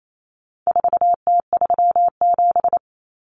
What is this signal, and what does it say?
Morse code, heard as a keyed tone, about 28 wpm, 700 Hz. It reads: 4T37
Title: Common Cut Numbers - 4-Digit 28wpm